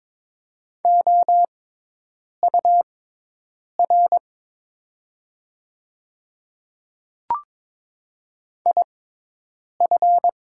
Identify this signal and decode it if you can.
Morse code, heard as a keyed tone, about 21 wpm, spaced as for 8 wpm, 700 Hz. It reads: OUR IF
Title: 6x Character Spacing - Top 200 Words - 22wpm